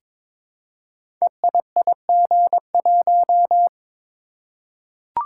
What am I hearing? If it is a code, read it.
Morse code, heard as a keyed tone, about 23 wpm, 700 Hz. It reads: EIIG1
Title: Code Groups - 5 Letter:Numbers - 22wpm